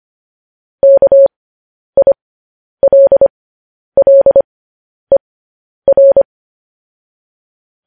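Morse code, heard as a keyed tone, about 25 wpm, spaced as for 10 wpm, 550 Hz. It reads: KILLER
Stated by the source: Morse code practice sending news headlines at 10 wpm.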